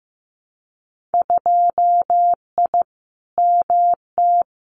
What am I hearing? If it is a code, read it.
Morse code, heard as a keyed tone, about 15 wpm, 700 Hz. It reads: 2I MT
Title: Winter Field Day Exchanges Encore 15wpm